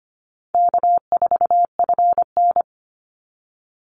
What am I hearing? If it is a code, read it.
Morse code, heard as a keyed tone, about 25 wpm, 700 Hz. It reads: K4FN
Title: Easy Call Signs Repeated 3x - 25wpm